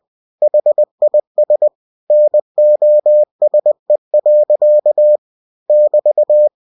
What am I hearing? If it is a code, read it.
Morse code, heard as a keyed tone, about 20 wpm, 600 Hz. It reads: HIS NOSE. <BT>
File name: cw-books / winnie-the-pooh / winnie-20wpm-page-0045